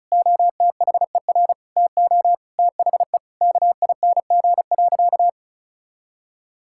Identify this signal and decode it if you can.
Morse code, heard as a keyed tone, about 34 wpm, 700 Hz. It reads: OTHER TO THE KING.
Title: Sentences from Top 1000 Words - Review Part 6 - 35wpm